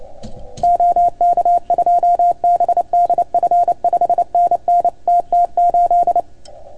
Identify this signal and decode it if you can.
Morse code, heard as a keyed tone, about 28 wpm, 690 Hz. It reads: OK2BDF 5NN TT8
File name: Archiv / CUC310 / OK1FGD CUC 310 pileup